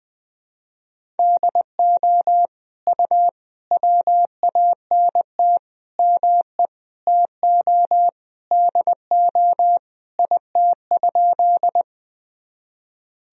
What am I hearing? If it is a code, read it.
Morse code, heard as a keyed tone, about 20 wpm, 700 Hz. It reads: DO U WANT ME TO DO IT?
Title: Sentences from Top 200 Words and Abbreviations 20wpm